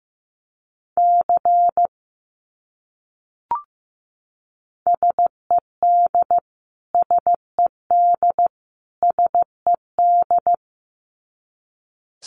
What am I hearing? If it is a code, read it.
Morse code, heard as a keyed tone, about 14 wpm, 700 Hz. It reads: C SED SED SED
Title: Abbreviations Repeated 3x - 15wpm